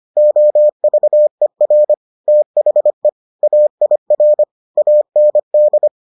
Morse code, heard as a keyed tone, about 25 wpm, 600 Hz. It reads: OVER THE AIR AND